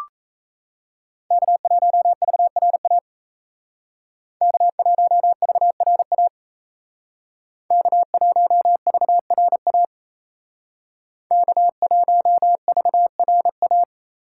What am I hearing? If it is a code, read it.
Morse code, heard as a keyed tone, about 36 wpm, 700 Hz. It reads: K1VRA K1VRA K1VRA K1VRA